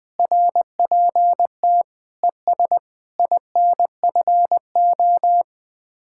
Morse code, heard as a keyed tone, about 20 wpm, 700 Hz. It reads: RPT ES INFO
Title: Speed-Racing Common QSO Phrases Encore - 17wpm